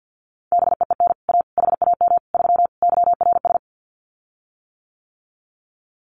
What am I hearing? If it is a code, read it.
Morse code, heard as a keyed tone, about 50 wpm, 700 Hz. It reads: BEEN A HAM 3 YRS
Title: Common QSO Phrases Encore 50wpm